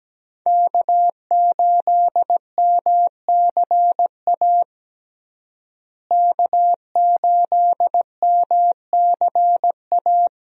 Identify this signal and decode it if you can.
Morse code, heard as a keyed tone, about 17 wpm, 700 Hz. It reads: K8MCA K8MCA
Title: USA Call Signs - 5 Characters - Repeated 2x - 17wpm